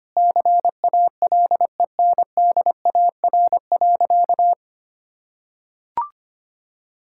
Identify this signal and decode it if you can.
Morse code, heard as a keyed tone, about 25 wpm, 700 Hz. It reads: CALENDAR.
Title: Sentences from Top 1600 Words 25wpm